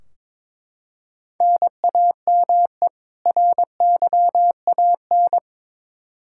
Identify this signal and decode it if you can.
Morse code, heard as a keyed tone, about 22 wpm, 700 Hz. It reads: NAME RYAN
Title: K1USN SST QSO Practice 22wpm